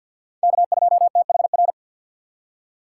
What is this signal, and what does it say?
Morse code, heard as a keyed tone, about 50 wpm, 700 Hz. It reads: K2THR
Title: USA Call Signs - 5 Characters - 4x Character Spacing - 50wpm